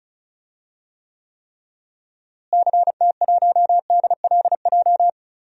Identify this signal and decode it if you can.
Morse code, heard as a keyed tone, about 34 wpm, 700 Hz. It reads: CT1DLJ
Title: DXCC-100 Call Signs - 6 Characters - Repeated 3x - 35wpm